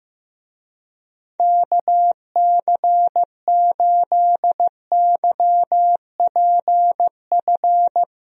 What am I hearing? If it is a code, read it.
Morse code, heard as a keyed tone, about 15 wpm, 700 Hz. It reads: KC8YPF
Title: USA Call Signs - 6 Characters - Repeated 3x - 15wpm